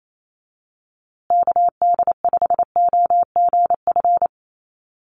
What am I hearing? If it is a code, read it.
Morse code, heard as a keyed tone, about 28 wpm, 700 Hz. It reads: KD5OGF
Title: USA Call Signs Repeated 2x 28wpm